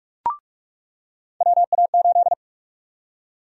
Morse code, heard as a keyed tone, about 42 wpm, 700 Hz. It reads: WA8
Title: USA Call Sign Prefixes - 45wpm